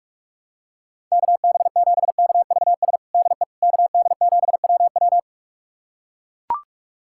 Morse code, heard as a keyed tone, about 41 wpm, 700 Hz. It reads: KB7KVS DE KD7WW
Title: USA Callsign DE Callsign 45wpm